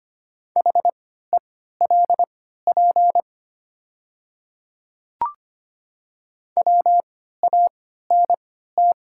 Morse code, heard as a keyed tone, about 25 wpm, spaced as for 14 wpm, 690 Hz. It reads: HELP WANT
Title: 3x Character Spacing - Top 300 Words - 25wpm